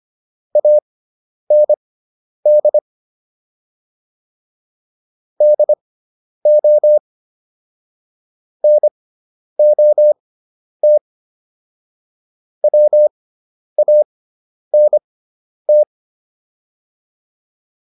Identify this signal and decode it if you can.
Morse code, heard as a keyed tone, about 25 wpm, spaced as for 10 wpm, 600 Hz. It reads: AND DO NOT WANT